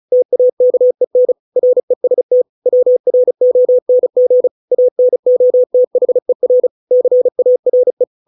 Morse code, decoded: TAKEN REST WRONG ANOTHER CARE

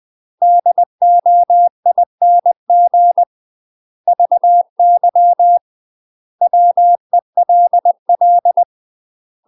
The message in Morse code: DOING VY WELL

